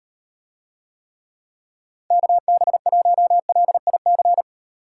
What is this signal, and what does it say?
Morse code, heard as a keyed tone, about 38 wpm, 700 Hz. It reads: KB1LIC